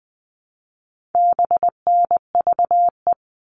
Morse code, heard as a keyed tone, about 20 wpm, 700 Hz. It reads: BNVE
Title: All Cut Numbers - 4-Digit 20wpm